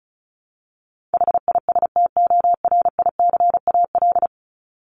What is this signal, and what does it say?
Morse code, heard as a keyed tone, about 34 wpm, 700 Hz. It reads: HISTORICAL
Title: Top 501-750 Words 35wpm